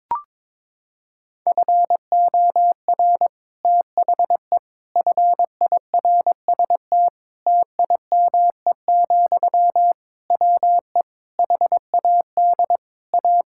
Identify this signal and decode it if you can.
Morse code, heard as a keyed tone, about 22 wpm, 700 Hz. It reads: FOR THE FIRST TIME, WE HAD A
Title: Sentences from Top 1000 Words - Review Part 6 - 22wpm